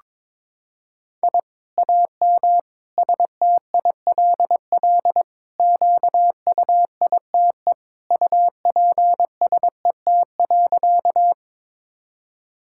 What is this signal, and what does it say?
Morse code, heard as a keyed tone, about 22 wpm, 700 Hz. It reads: I AM STILL QUITE UPSET.
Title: Sentences from Top 1000 Words - Review Part 5 - 22wpm